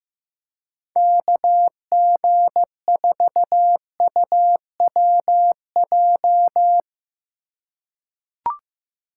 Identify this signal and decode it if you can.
Morse code, heard as a keyed tone, about 15 wpm, 700 Hz. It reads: KG4UWJ
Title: USA Call Signs - 6 Characters - 2x Character Spacing - 15wpm